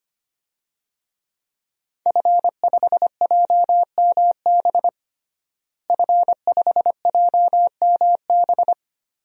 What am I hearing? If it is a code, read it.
Morse code, heard as a keyed tone, about 25 wpm, 700 Hz. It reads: F5JMB F5JMB
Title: DXCC-100 Call Signs - 5 Characters - Repeated 2x - 25wpm